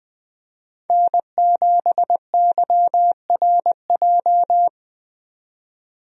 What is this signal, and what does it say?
Morse code, heard as a keyed tone, about 20 wpm, 700 Hz. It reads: N7YRJ
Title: USA Call Signs - 5 Characters - 20wpm